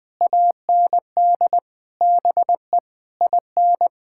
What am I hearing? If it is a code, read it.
Morse code, heard as a keyed tone, about 20 wpm, 700 Hz. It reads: AND BE IN
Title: Sentences from Top 1000 Words - Review Part 6 - 20wpm